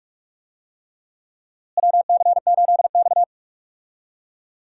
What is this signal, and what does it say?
Morse code, heard as a keyed tone, about 45 wpm, 700 Hz. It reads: WC8X
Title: USA Call Signs - 4 Characters - Repeated 3x - 45wpm